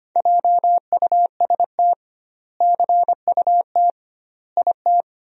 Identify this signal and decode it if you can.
Morse code, heard as a keyed tone, about 25 wpm, 700 Hz. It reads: JUST CUT IT
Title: 2x Word-Spacing - Sentences from Top 300 Words - 25wpm